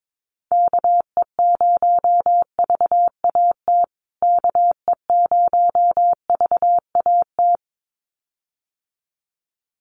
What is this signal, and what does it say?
Morse code, heard as a keyed tone, about 22 wpm, 700 Hz. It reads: KE0VAT KE0VAT
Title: USA Call Signs Repeated 2x 22wpm